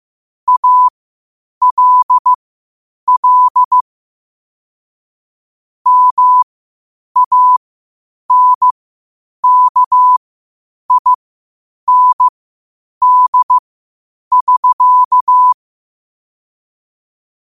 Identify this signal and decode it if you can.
Morse code, heard as a keyed tone, about 15 wpm, spaced as for 8 wpm, 1000 Hz. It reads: ALL MANKIND<SK>
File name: qotdfiles / 8wpm